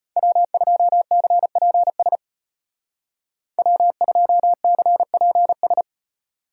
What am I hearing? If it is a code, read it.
Morse code, heard as a keyed tone, about 34 wpm, 700 Hz. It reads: W2CPS W2CPS